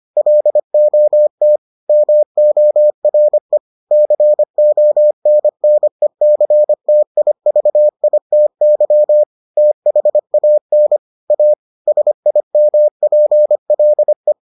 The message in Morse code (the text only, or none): LOT MORE CONNECTIVITY THAN A SIMPLE